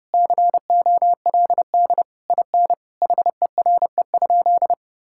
Morse code, heard as a keyed tone, about 30 wpm, 700 Hz. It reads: COLD IN HERE?
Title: Sentences from Top 1000 Words - Review Part 6 - 30wpm